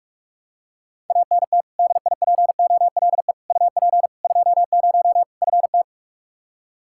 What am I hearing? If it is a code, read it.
Morse code, heard as a keyed tone, about 42 wpm, 700 Hz. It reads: ANT DIPOLE UP 20 FT
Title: Practice QSO - Exchange 2 - Info - 45wpm